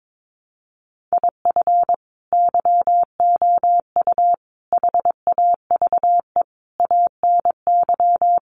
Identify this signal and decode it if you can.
Morse code, heard as a keyed tone, about 22 wpm, 700 Hz. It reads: IF YOU HAVE ANY
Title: Sentences from Top 1100 Words 22wpm